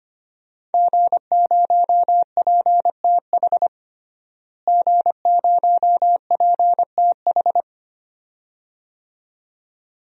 Morse code, heard as a keyed tone, about 25 wpm, 700 Hz. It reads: G0PTH G0PTH
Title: DXCC-100 Call Signs - 5 Characters - Repeated 2x - 25wpm